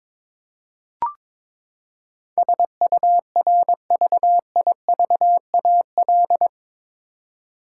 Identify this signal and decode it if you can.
Morse code, heard as a keyed tone, about 22 wpm, 700 Hz. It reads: SURVIVAL
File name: practice / 22wpm / Eight Letter Words - 22wpm